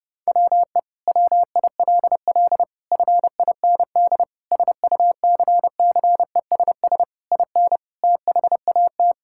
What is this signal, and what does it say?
Morse code, heard as a keyed tone, about 30 wpm, 700 Hz. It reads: WE WILL FIND SUCCESS IN THAT
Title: Sentences from Top 1000 Words - Review Part 5 - 30wpm